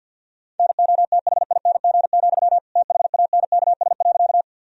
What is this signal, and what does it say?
Morse code, heard as a keyed tone, about 51 wpm, 700 Hz. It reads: NOTHING, THANKS.
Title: Sentences from Top 1000 Words - Review Part 3 - 50wpm